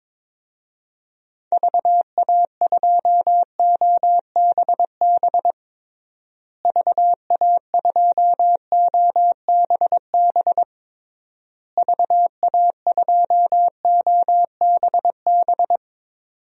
Morse code, heard as a keyed tone, about 22 wpm, 700 Hz. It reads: VA2OBB VA2OBB VA2OBB
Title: DXCC-100 Call Signs - 6 Characters - Repeated 3x - 22wpm